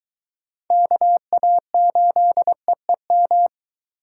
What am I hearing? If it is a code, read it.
Morse code, heard as a keyed tone, about 23 wpm, 700 Hz. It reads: KA8EEM